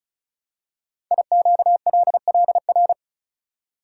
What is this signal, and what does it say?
Morse code, heard as a keyed tone, about 36 wpm, 700 Hz. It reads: IQLLR
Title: Code Groups - 5 Letters - 35wpm